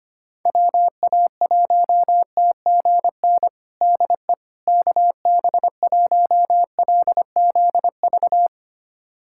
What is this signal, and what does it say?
Morse code, heard as a keyed tone, about 25 wpm, 700 Hz. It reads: WA1TGN DE KB1LZV
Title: USA Callsign DE Callsign - 6 Characters - 25wpm